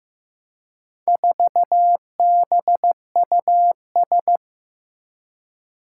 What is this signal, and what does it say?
Morse code, heard as a keyed tone, about 15 wpm, 700 Hz. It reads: 4BUS
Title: Four Challenging Letter-Number Combinations 15wpm